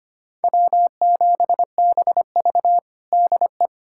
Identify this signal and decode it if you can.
Morse code, heard as a keyed tone, about 25 wpm, 700 Hz. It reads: W7BV DE
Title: USA Callsign DE Callsign - 4 Characters - 25wpm